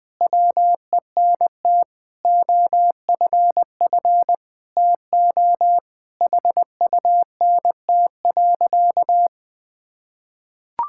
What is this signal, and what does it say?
Morse code, heard as a keyed tone, about 20 wpm, 700 Hz. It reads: WENT OFF TO HUNT.
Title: Sentences from Top 1600 Words 20wpm